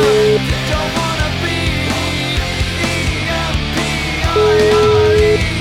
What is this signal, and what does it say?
Ringback tone (Europe, 425 Hz), with noise.